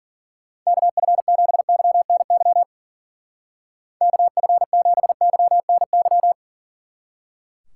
Morse code, heard as a keyed tone, about 49 wpm, 700 Hz. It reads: KF7YNY KF7YNY
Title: Speed Racing - USA 2x3 Call Signs 40wpm